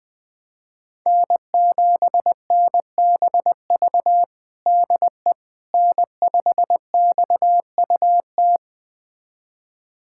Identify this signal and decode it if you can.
Morse code, heard as a keyed tone, about 20 wpm, 700 Hz. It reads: N7NBV DE N5XUT
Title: USA Callsign DE Callsign - 5 Characters - 20wpm